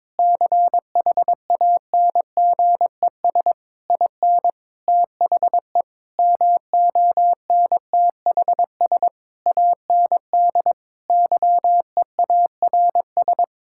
Morse code, heard as a keyed tone, about 21 wpm, 700 Hz. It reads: CHANGES IN THE MONTHS AND YEARS
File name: Sentences with 7-Word Phrases - 22wpm